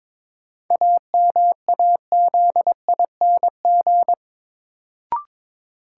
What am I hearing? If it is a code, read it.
Morse code, heard as a keyed tone, about 22 wpm, 700 Hz. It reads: AMAZING
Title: Sentences from Top 1500 Words - New Word Drill - 22wpm